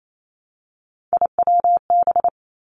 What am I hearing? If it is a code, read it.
Morse code, heard as a keyed tone, about 28 wpm, 700 Hz. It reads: IWB
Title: Three Challenging Letter-Number Combinations 28wpm